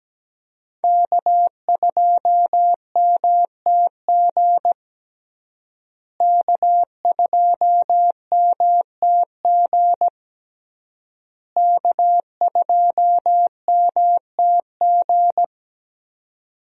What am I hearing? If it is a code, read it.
Morse code, heard as a keyed tone, about 17 wpm, 700 Hz. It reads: K2MTG K2MTG K2MTG
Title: USA Call Signs - 5 Characters - Repeated 3x - 17wpm